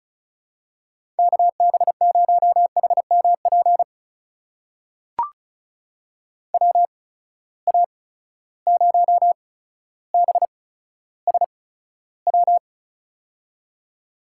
Morse code, heard as a keyed tone, about 36 wpm, spaced as for 15 wpm, 700 Hz. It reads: KB0HMP WA0BSW
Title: USA Call Signs - 6 Characters - 8x Character Spacing - 35wpm